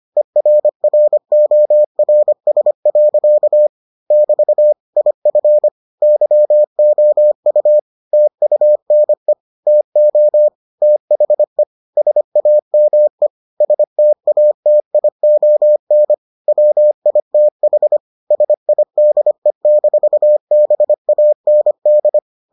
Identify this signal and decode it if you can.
Morse code, heard as a keyed tone, about 25 wpm, 600 Hz. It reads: ERRORS. <BT> IF YOU TUNE TO THE SAME STATION WITH SIDE-BAND